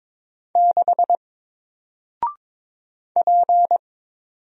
Morse code, heard as a keyed tone, about 22 wpm, 700 Hz. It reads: 6 P
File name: Single-Letter-Number-22wpm